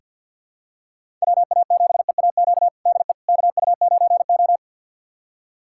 Morse code, heard as a keyed tone, about 50 wpm, 700 Hz. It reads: WA7EAQ DE KU9O
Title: USA Callsign DE Callsign 50wpm